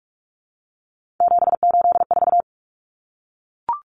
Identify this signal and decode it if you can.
Morse code, heard as a keyed tone, about 44 wpm, 700 Hz. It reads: 784
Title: Common Cut Numbers - 3-Digit 45wpm